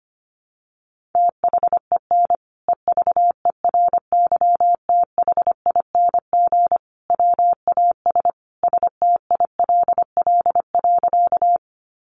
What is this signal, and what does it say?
Morse code, heard as a keyed tone, about 25 wpm, 700 Hz. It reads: THEN EVERYTHING WAS STILL.
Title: Sentences from Top 700 Words 25wpm